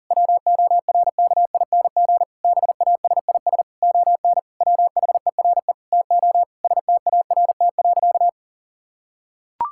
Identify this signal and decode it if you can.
Morse code, heard as a keyed tone, about 40 wpm, 700 Hz. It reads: WORKING BASIS ON WHERE TO START.
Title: Sentences from Top 1000 Words - Review Part 5 - 40wpm